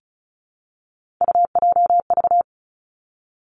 Morse code, heard as a keyed tone, about 36 wpm, 700 Hz. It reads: UJV
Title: Three Challenging Letter-Number Combinations 35wpm